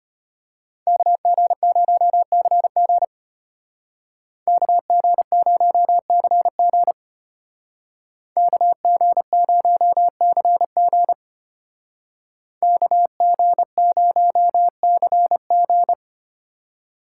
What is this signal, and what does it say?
Morse code, heard as a keyed tone, about 30 wpm, 700 Hz. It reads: KG0CG KG0CG KG0CG KG0CG
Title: Speed Racing - USA Call Signs 25wpm